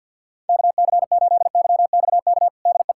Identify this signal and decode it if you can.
Morse code, heard as a keyed tone, about 52 wpm, 700 Hz. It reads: KC8YXK DE
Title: USA Callsign DE Callsign - 6 Characters - 50wpm